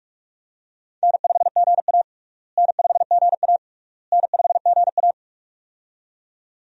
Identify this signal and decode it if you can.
Morse code, heard as a keyed tone, about 42 wpm, 700 Hz. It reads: N5GA N5GA N5GA
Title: USA Call Signs - 4 Characters - Repeated 3x - 45wpm